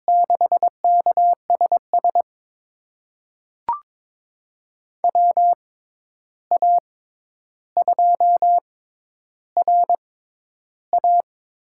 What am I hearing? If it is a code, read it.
Morse code, heard as a keyed tone, about 22 wpm, spaced as for 11 wpm, 700 Hz. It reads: 6KSS WA2RA
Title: USA Call Signs - 6 Characters - 6x Character Spacing - 22wpm